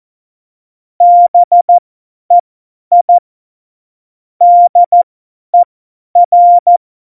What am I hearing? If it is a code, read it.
Morse code, heard as a keyed tone, about 14 wpm, spaced as for 10 wpm, 700 Hz. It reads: BEI DER